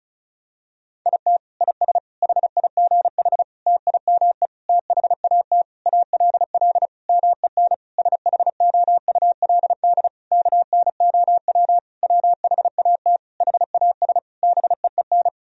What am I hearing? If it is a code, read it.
Morse code, heard as a keyed tone, about 36 wpm, 700 Hz. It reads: IT IS HIGH TIME THAT ALL MEN SHOULD KNOW WHAT HAS BEEN
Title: Sentences from Top 1000 Words - Review Part 1 - 35wpm